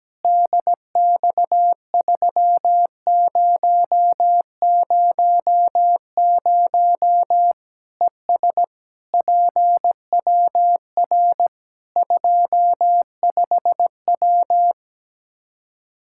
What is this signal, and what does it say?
Morse code, heard as a keyed tone, about 17 wpm, 700 Hz. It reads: DX3000 ES PWR 25W
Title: K1USN SST QSO Practice 17wpm